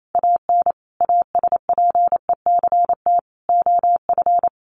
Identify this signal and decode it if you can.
Morse code, heard as a keyed tone, about 28 wpm, 700 Hz. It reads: AN ASPECT OF
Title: Sentences from Top 1300 Words 28wpm